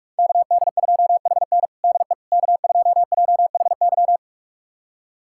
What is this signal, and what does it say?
Morse code, heard as a keyed tone, about 41 wpm, 700 Hz. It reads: KD2HN DE K2JHY
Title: USA Callsign DE Callsign - 5 Characters - 45wpm